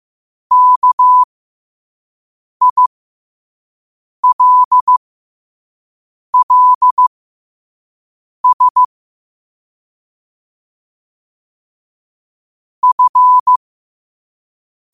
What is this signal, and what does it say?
Morse code, heard as a keyed tone, about 15 wpm, spaced as for 5 wpm, 1000 Hz. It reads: KILLS F
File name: qotdfiles / 5wpm